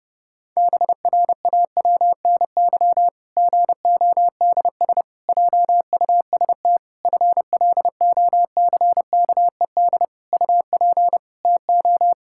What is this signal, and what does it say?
Morse code, heard as a keyed tone, about 32 wpm, 700 Hz. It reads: BRAWNY GODS JUST FLOCKED UP TO
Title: Pangram Challenge v2 - 30wpm